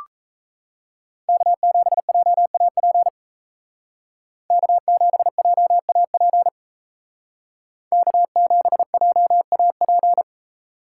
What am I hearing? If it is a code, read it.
Morse code, heard as a keyed tone, about 36 wpm, 700 Hz. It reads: K7JAP K7JAP K7JAP